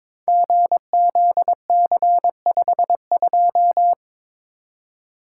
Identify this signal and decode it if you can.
Morse code, heard as a keyed tone, about 22 wpm, 700 Hz. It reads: GZC52
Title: Code Groups - 5 Letter:Numbers - 22wpm